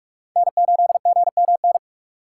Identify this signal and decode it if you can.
Morse code, heard as a keyed tone, about 47 wpm, 700 Hz. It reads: N8GMN
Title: USA Call Signs - 2x Character Spacing - 45wpm